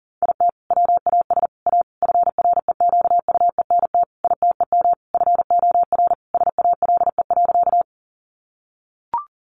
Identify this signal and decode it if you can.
Morse code, heard as a keyed tone, about 40 wpm, 700 Hz. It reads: IT WAS A FREQUENT ITEM FOR SALE.
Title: Sentences from Top 1400 Words 40wpm